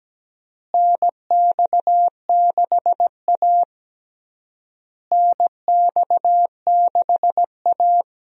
Morse code, heard as a keyed tone, about 17 wpm, 700 Hz. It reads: NX6A NX6A
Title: USA Call Signs - 4 Characters - Repeated 3x - 17wpm